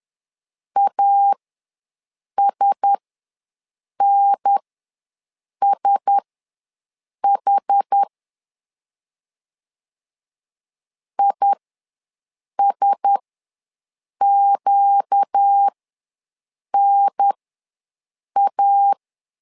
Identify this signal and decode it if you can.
Morse code, heard as a keyed tone, about 10 wpm, spaced as for 6 wpm, 800 Hz. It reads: ASNSH ISQNA
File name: LEZIONI / 4 Q1S ws 10wpm 2208